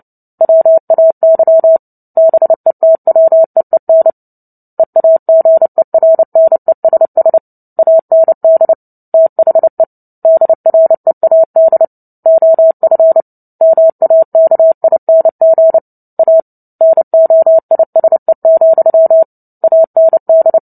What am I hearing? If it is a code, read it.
Morse code, heard as a keyed tone, about 29 wpm, 650 Hz. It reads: WAY BETWEEN EAGERNESS AND THE DREAD OF MAKING A NOISE, AND
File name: Book Two / 03-Chapter 3 - 29 WPM